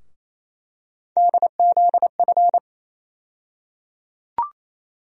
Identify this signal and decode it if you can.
Morse code, heard as a keyed tone, about 28 wpm, 700 Hz. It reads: DZF
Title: USA Call Sign Postfixes - 3 Letters 28wpm